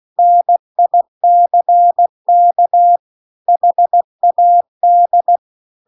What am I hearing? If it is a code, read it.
Morse code, heard as a keyed tone, about 16 wpm, 700 Hz. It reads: NICK HAD